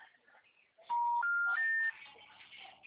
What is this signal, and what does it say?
Special information tone.